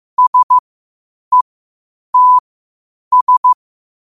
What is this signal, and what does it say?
Morse code, heard as a keyed tone, about 15 wpm, spaced as for 8 wpm, 1000 Hz. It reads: SETS
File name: qotdfiles / 8wpm